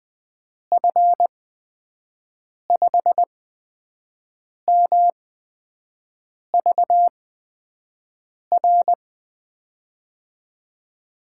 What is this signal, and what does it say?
Morse code, heard as a keyed tone, about 20 wpm, spaced as for 5 wpm, 700 Hz. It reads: F5MVR
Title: USA Call Signs - 6 Characters - 8x Character Spacing - 20wpm